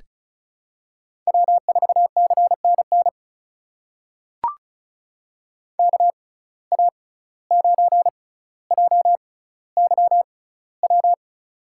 Morse code, heard as a keyed tone, about 36 wpm, spaced as for 17 wpm, 700 Hz. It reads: W4CNN KA9JYW
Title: USA Call Signs - 6x Character Spacing - 35wpm